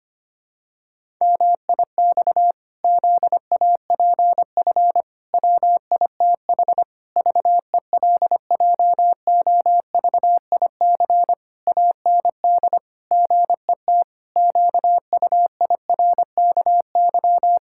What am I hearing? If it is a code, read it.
Morse code, heard as a keyed tone, about 25 wpm, 700 Hz. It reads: MIX ZAPF WITH VELJOVIC AND GET QUIRKY
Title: Pangram Challenge v2 - 25wpm